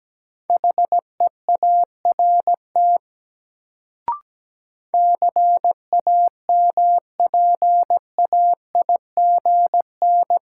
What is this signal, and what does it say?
Morse code, heard as a keyed tone, about 17 wpm, 700 Hz. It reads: HEART CAMPAIGN
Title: Sentences from Top 1200 Words - New Word Drill Alt - 17wpm